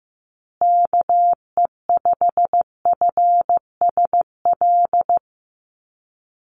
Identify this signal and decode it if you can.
Morse code, heard as a keyed tone, about 14 wpm, 700 Hz. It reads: KE5FSL
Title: USA Call Signs Repeated 3x 15wpm